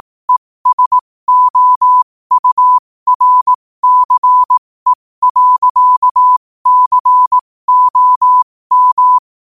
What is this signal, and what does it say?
Morse code, heard as a keyed tone, about 18 wpm, spaced as for 15 wpm, 1000 Hz. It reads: ESOURCE.COM
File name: qotdfiles / 15wpm